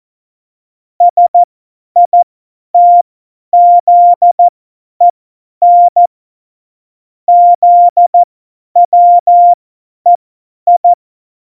Morse code, heard as a keyed tone, about 14 wpm, spaced as for 10 wpm, 700 Hz. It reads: SITZEN ZWEI
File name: Morsekurs / Witze / Witz 14. August 2018, 50 F70